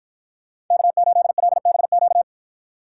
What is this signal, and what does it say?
Morse code, heard as a keyed tone, about 53 wpm, 700 Hz. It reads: K8LBQ